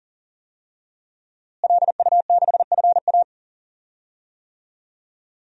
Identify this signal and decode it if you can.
Morse code, heard as a keyed tone, about 40 wpm, 700 Hz. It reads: LU6FA
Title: DXCC-100 Call Signs - 5 Characters - Repeated 2x - 40wpm